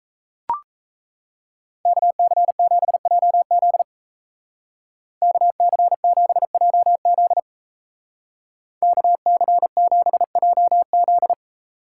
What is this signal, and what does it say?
Morse code, heard as a keyed tone, about 37 wpm, 700 Hz. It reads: KC7JZ KC7JZ KC7JZ